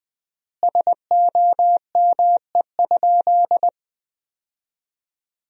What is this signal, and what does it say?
Morse code, heard as a keyed tone, about 20 wpm, 700 Hz. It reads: SOME?
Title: 3x Word-Spacing - Sentences from Top 100 Words - 20wpm